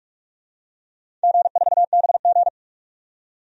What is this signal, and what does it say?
Morse code, heard as a keyed tone, about 46 wpm, 700 Hz. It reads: G4BG
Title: DXCC-100 Call Signs - 4 Characters - 45wpm